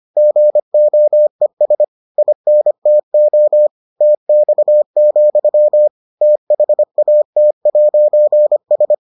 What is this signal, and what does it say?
Morse code, heard as a keyed tone, about 25 wpm, 600 Hz. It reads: GOES INTO TX, THAT'S